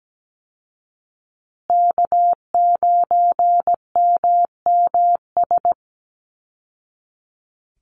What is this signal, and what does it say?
Morse code, heard as a keyed tone, about 17 wpm, 700 Hz. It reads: K9MMS
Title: Call Signs Repeated 2x - 17wpm